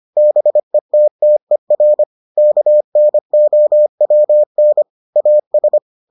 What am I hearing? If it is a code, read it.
Morse code, heard as a keyed tone, about 25 wpm, 600 Hz. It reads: BETTER KNOWN AS